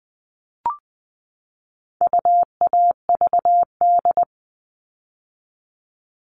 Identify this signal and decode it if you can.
Morse code, heard as a keyed tone, about 20 wpm, 700 Hz. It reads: UAVD
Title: All Cut Numbers - 4-Digit 20wpm